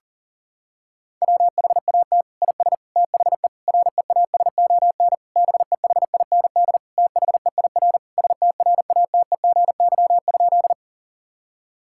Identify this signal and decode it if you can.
Morse code, heard as a keyed tone, about 40 wpm, 700 Hz. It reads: WHAT IS THE REASON BEHIND THEIR STRATEGY?
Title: Sentences from Top 1000 Words - Review Part 5 - 40wpm